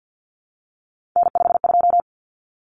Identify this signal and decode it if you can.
Morse code, heard as a keyed tone, about 50 wpm, 700 Hz. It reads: N52
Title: Common Cut Numbers - 3-Digit 50wpm